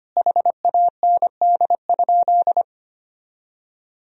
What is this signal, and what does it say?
Morse code, heard as a keyed tone, about 25 wpm, 700 Hz. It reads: HAND?
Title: Sentences from Top 1000 Words - Review Part 4 - 25wpm